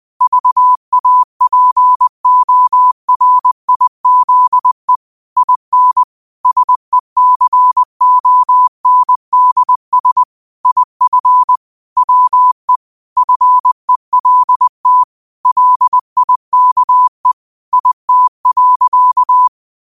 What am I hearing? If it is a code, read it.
Morse code, heard as a keyed tone, about 20 wpm, 1000 Hz. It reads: VAPORIZE IN SECONDS IF WE FELT LIKE IT.
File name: qotdfiles / 20wpm